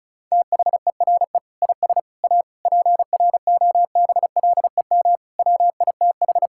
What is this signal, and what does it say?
Morse code, heard as a keyed tone, about 36 wpm, 700 Hz. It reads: THERE IS A PROBLEM WITH
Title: Sentences from Top 1000 Words - Review Part 5 - 35wpm